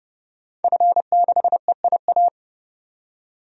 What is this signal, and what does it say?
Morse code, heard as a keyed tone, about 32 wpm, 700 Hz. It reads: F6EIA
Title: DXCC-100 Call Signs - 5 Characters - 30wpm